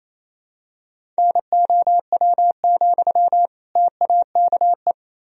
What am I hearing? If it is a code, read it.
Morse code, heard as a keyed tone, about 28 wpm, 700 Hz. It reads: NOW, TAKE
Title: Sentences from Top 400 Words and Abbreviations 28wpm